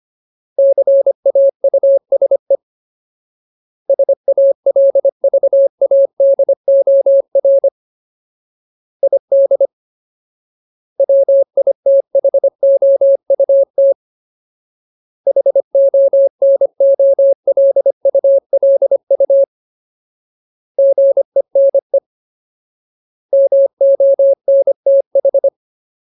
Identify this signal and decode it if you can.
Morse code, heard as a keyed tone, about 25 wpm, 550 Hz. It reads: CAUSE SALVADOR ID WITHOUT HONOLULU GENE MONTH
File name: Ultimate Word List / 25WPM Ultimate